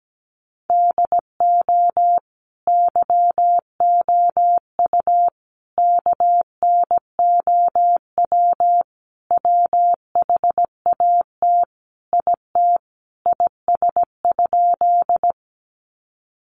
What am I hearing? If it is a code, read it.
Morse code, heard as a keyed tone, about 17 wpm, 700 Hz. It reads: DO YOU KNOW WHAT IT IS?
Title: Sentences from Top 100 Words 17wpm